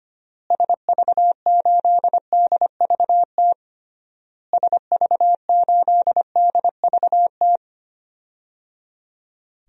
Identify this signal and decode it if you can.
Morse code, heard as a keyed tone, about 25 wpm, 700 Hz. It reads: SV8DVT SV8DVT
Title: DXCC-100 Call Signs - 6 Characters - Repeated 3x - 25wpm